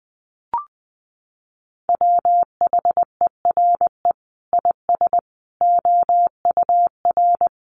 Morse code, heard as a keyed tone, about 20 wpm, 700 Hz. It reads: WHERE IS OUR
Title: Sentences from Top 1300 Words 20wpm